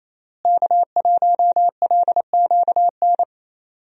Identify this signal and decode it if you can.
Morse code, heard as a keyed tone, about 28 wpm, 700 Hz. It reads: K1LQN
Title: USA Call Signs - 5 Characters - Repeated 2x - 28wpm